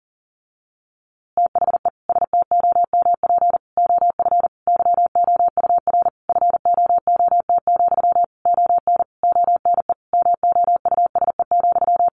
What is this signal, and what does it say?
Morse code, heard as a keyed tone, about 40 wpm, 700 Hz. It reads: THE STOMP OF YOUR FOOT, ON ONE MOUSE,
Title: A Sound Of Thunder 40wpm